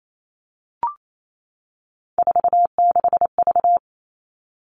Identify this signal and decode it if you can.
Morse code, heard as a keyed tone, about 28 wpm, 700 Hz. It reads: 46V
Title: All Cut Numbers - 3-Digit 28wpm